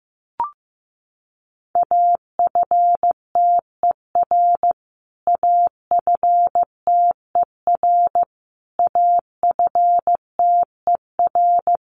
Morse code, heard as a keyed tone, about 15 wpm, 700 Hz. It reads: AFTER AFTER AFTER
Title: Top 200 Words Repeated 15wpm